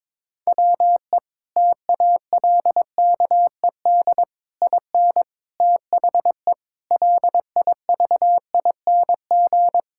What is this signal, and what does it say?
Morse code, heard as a keyed tone, about 21 wpm, 700 Hz. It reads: WE TALKED IN THE LIVING
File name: practice / 22wpm / Sentences with 4-Word Phrases - 22wpm